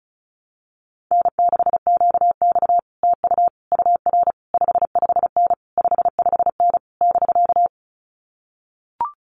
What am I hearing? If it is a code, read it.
Morse code, heard as a keyed tone, about 36 wpm, 700 Hz. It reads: N6QX TU UR 55N 55N <BK>
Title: Common QSO Phrases Encore 35wpm